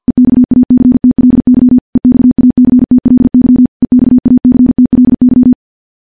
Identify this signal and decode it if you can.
Morse code, heard as a keyed tone, about 51 wpm, 250 Hz. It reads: EXACTLY EXACTLY EXACTLY
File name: morse / morse-training-50WPM